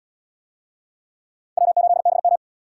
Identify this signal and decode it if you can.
Morse code, heard as a keyed tone, about 78 wpm, 700 Hz. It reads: W8BM